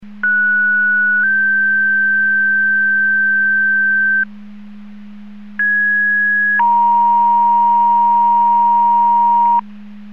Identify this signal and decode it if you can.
Two-tone page, heard as a one-second tone, then a three-second tone, about 1470 then 1550 Hz, about 1650 then 940 Hz. It is heard from 0.2 to 9.6 s.
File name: Portage 154.385 Paging